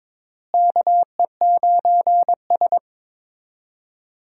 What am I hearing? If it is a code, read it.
Morse code, heard as a keyed tone, about 22 wpm, 700 Hz. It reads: KE9S
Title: USA Call Signs - 4 Characters - Repeated 2x - 22wpm